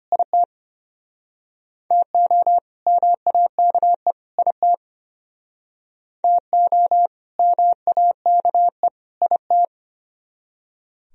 Morse code, heard as a keyed tone, about 32 wpm, 700 Hz. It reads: IT TO MAKE IT TO MAKE IT